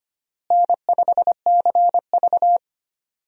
Morse code, heard as a keyed tone, about 25 wpm, 700 Hz. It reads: N5CV
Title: USA Call Signs - 4 Characters - Repeated 3x - 25wpm